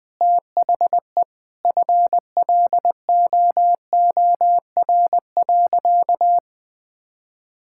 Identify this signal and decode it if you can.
Morse code, heard as a keyed tone, about 20 wpm, 700 Hz. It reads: THE FLOOR.
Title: Sentences with 3-Word Phrases - 20wpm